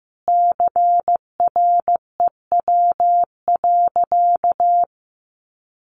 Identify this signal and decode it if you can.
Morse code, heard as a keyed tone, about 15 wpm, 700 Hz. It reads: CREW.
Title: Sentences from Top 1500 Words 15wpm